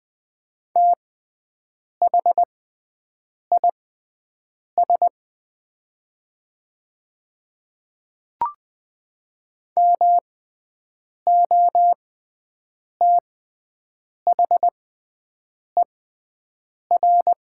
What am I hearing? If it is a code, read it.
Morse code, heard as a keyed tone, about 20 wpm, spaced as for 7 wpm, 690 Hz. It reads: THIS MOTHER
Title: 6x Character Spacing - Top 500 Words - 20wpm